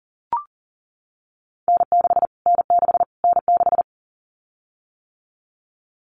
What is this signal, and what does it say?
Morse code, heard as a keyed tone, about 40 wpm, 700 Hz. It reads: N6 N6 N6
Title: Callsign Prefixes for Top 25 DXCC - Repeated 3x - 40wpm